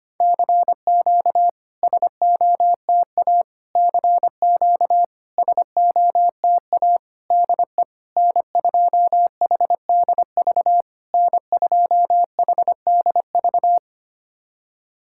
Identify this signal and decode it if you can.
Morse code, heard as a keyed tone, about 25 wpm, 700 Hz. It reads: CQ SOTA CQ SOTA DE N2HDV N2HDV
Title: SOTA QSO Practice 25wpm